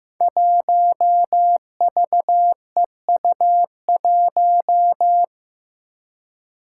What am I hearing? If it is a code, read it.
Morse code, heard as a keyed tone, about 15 wpm, 700 Hz. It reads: 1VEU1
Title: Five Challenging Letter-Number Combinations 15wpm